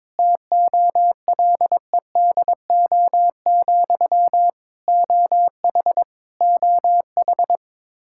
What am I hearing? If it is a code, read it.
Morse code, heard as a keyed tone, about 22 wpm, 700 Hz. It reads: TOLEDO, OH OH
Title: Practice QSO - Exchange 1 - Beginning - 22wpm